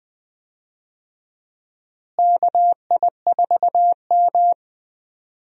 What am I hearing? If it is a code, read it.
Morse code, heard as a keyed tone, about 20 wpm, 700 Hz. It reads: KI4M
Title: USA Call Signs - 4 Characters - Repeated 2x - 20wpm